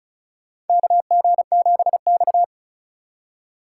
Morse code, heard as a keyed tone, about 34 wpm, 700 Hz. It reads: KG7X
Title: USA Call Signs - 4 Characters - 8x Character Spacing - 35wpm